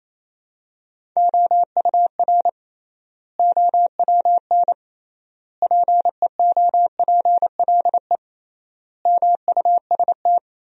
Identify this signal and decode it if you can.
Morse code, heard as a keyed tone, about 27 wpm, 700 Hz. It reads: OUR OWN PEOPLE MUST
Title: 3x Word-Spacing - Sentences from Top 300 Words - 28wpm